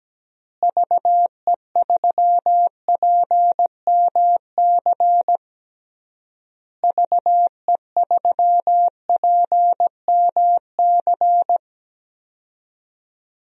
Morse code, heard as a keyed tone, about 17 wpm, 700 Hz. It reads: VE3PMC VE3PMC
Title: DXCC-100 Call Signs - 6 Characters - Repeated 3x - 17wpm